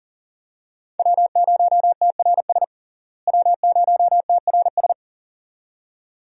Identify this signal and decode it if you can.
Morse code, heard as a keyed tone, about 40 wpm, 700 Hz. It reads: W0TRS W0TRS
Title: USA Call Signs - 5 Characters - Repeated 2x - 40wpm